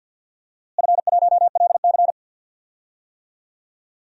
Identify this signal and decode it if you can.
Morse code, heard as a keyed tone, about 52 wpm, 700 Hz. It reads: F1LC
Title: DXCC-100 Call Signs - 4 Characters - Repeated 3x - 50wpm